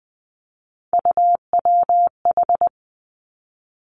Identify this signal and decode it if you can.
Morse code, heard as a keyed tone, about 20 wpm, 700 Hz. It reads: UWH
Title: Three Challenging Letter-Number Combinations 20wpm